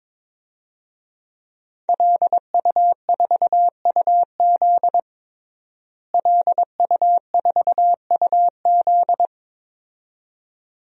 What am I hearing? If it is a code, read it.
Morse code, heard as a keyed tone, about 22 wpm, 700 Hz. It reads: LU4UZ LU4UZ
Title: DXCC-100 Call Signs - 5 Characters - Repeated 2x - 22wpm